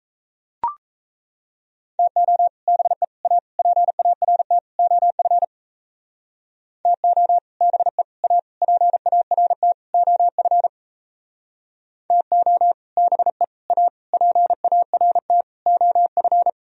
Morse code, heard as a keyed tone, about 34 wpm, 700 Hz. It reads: TO BE A PART OF TO BE A PART OF TO BE A PART OF
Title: Speed-Racing - Sets of 5 Words 28wpm